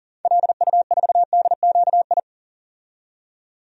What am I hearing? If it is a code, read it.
Morse code, heard as a keyed tone, about 40 wpm, 700 Hz. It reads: LU4DQI
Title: DXCC-100 Call Signs - 6 Characters - 40wpm